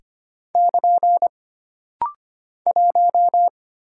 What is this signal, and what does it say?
Morse code, heard as a keyed tone, about 25 wpm, 700 Hz. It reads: <KN> 1
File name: Single Character (Letter:Number:Punctuation:Prosign) - Rapid-Fire - 25wpm